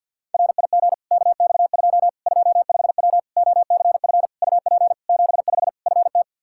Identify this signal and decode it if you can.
Morse code, heard as a keyed tone, about 53 wpm, 700 Hz. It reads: RIG KX2 25W OCF UP 75 FT
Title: K1USN SST QSO Practice 50wpm